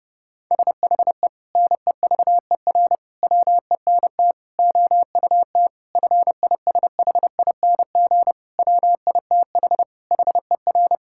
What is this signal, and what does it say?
Morse code, heard as a keyed tone, about 30 wpm, 700 Hz. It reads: SHE NEVER WENT OUT FISHING WITH HER
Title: Sentences from Top 1000 Words - Review Part 5 - 30wpm